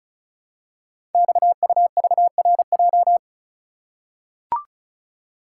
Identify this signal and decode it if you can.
Morse code, heard as a keyed tone, about 36 wpm, 690 Hz. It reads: XUVRJ
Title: Code Groups - 5 Letters - 35wpm